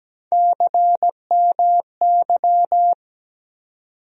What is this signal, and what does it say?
Morse code, heard as a keyed tone, about 17 wpm, 700 Hz. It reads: CMY
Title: USA Call Sign Postfixes - 3 Letters 17wpm